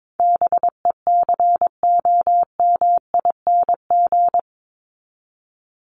Morse code, heard as a keyed tone, about 22 wpm, 700 Hz. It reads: BECOMING
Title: ING Suffix - Top 100 Words with Base Words 22wpm